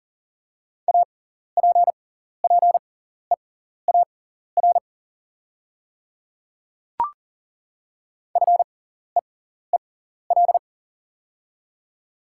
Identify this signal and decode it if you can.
Morse code, heard as a keyed tone, about 40 wpm, spaced as for 14 wpm, 700 Hz. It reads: APPEAR FEEL
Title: 6x Character Spacing - Top 500 Words - 40wpm